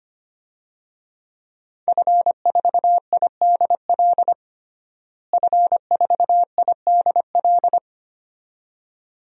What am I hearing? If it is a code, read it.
Morse code, heard as a keyed tone, about 25 wpm, 700 Hz. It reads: F4IDL F4IDL
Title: DXCC-100 Call Signs - 5 Characters - Repeated 2x - 25wpm